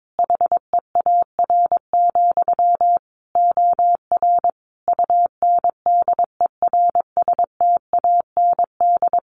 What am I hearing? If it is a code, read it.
Morse code, heard as a keyed tone, about 22 wpm, 700 Hz. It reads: HEAR, OR UNDERSTAND
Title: Sentences from Top 1300 Words 22wpm